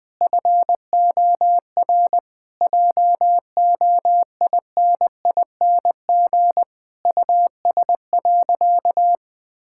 Morse code, heard as a keyed tone, about 19 wpm, 700 Hz. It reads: FOR JOINING US.
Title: Sentences with 5-Word Phrases - 20wpm